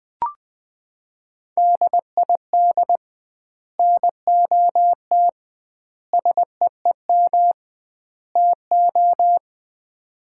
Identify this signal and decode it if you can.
Morse code, heard as a keyed tone, about 20 wpm, 700 Hz. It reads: DID NOT SEEM TO
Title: 2x Word-Spacing - Sets of 4 Words - 20wpm